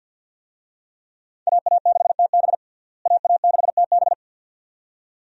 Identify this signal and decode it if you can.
Morse code, heard as a keyed tone, about 52 wpm, 700 Hz. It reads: AA6TB AA6TB
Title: USA Call Signs - 5 Characters - Repeated 2x - 50wpm